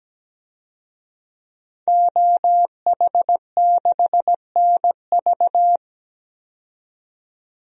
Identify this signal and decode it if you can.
Morse code, heard as a keyed tone, about 17 wpm, 700 Hz. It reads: OH6NV
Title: DXCC-100 Call Signs - 5 Characters - 17wpm